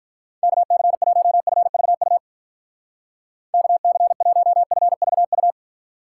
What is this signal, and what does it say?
Morse code, heard as a keyed tone, about 42 wpm, 700 Hz. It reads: KC1FVU KC1FVU
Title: Speed Racing - USA 2x3 Call Signs 35wpm